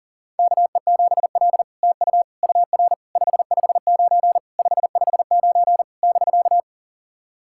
Morse code, heard as a keyed tone, about 38 wpm, 700 Hz. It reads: KE7L TU UR 559 559 <BK>
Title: SOTA QSO Practice 40wpm